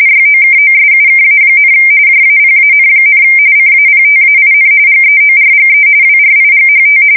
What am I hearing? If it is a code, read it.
RTTY, about 45.45 baud. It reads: YOKOSUKA, JAPAN, CAPTAIN GEORGE WRIGHT
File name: fsk170 / USS-Missouri